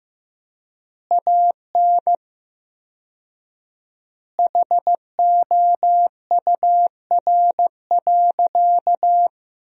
Morse code, heard as a keyed tone, about 15 wpm, 700 Hz. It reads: AN HOUR.
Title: 4x Word-Spacing - Sentences from Top 400 Words - 15wpm